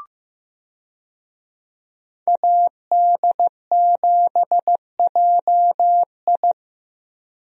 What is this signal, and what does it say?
Morse code, heard as a keyed tone, about 15 wpm, 700 Hz. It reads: AD7JI
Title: USA Call Signs - 5 Characters - Repeated 2x - 15wpm